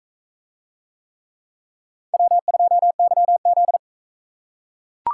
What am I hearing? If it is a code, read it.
Morse code, heard as a keyed tone, about 42 wpm, 700 Hz. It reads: W2YZ